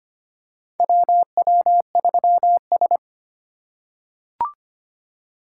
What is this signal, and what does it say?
Morse code, heard as a keyed tone, about 25 wpm, 700 Hz. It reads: WW3S
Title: USA Call Signs - 4 Characters - 4x Character Spacing - 25wpm